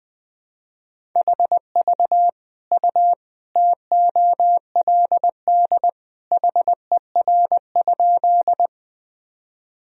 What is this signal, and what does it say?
Morse code, heard as a keyed tone, about 19 wpm, 700 Hz. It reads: HV U TOLD HER?
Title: Sentences from Top 400 Words and Abbreviations 20wpm